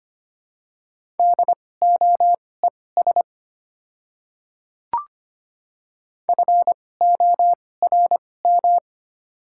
Morse code, heard as a keyed tone, about 25 wpm, spaced as for 18 wpm, 700 Hz. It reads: DOES FORM
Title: 2x Character Spacing - Top 200 Words - 25wpm